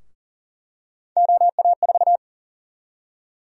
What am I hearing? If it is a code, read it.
Morse code, heard as a keyed tone, about 40 wpm, 700 Hz. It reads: OA4
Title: USA Call Sign Prefixes - Repeated 2x - 40wpm